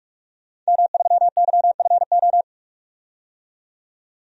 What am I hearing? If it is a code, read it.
Morse code, heard as a keyed tone, about 46 wpm, 700 Hz. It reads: M3YFO
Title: DXCC-100 Call Signs - 5 Characters - 45wpm